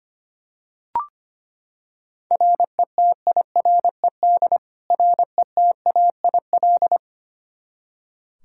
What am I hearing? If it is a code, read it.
Morse code, heard as a keyed tone, about 25 wpm, 700 Hz. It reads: RETIRED RETAIL
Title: K1USN SST QSO Practice 25wpm